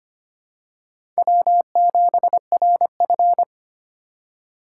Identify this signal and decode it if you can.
Morse code, heard as a keyed tone, about 25 wpm, 700 Hz. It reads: W7RF
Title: USA Call Signs - 4 Characters - 2x Character Spacing - 25wpm